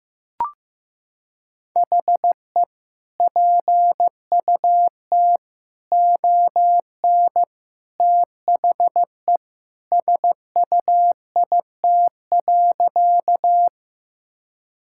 Morse code, heard as a keyed tone, about 15 wpm, 700 Hz. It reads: HE PUT ON THE SUIT.
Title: Sentences from Top 1600 Words 15wpm